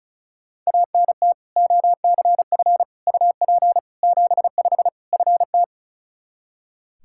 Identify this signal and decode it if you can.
Morse code, heard as a keyed tone, about 36 wpm, 700 Hz. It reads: ANT OCF UP 75 FT
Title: Practice QSO - Exchange 2 - Info - 35wpm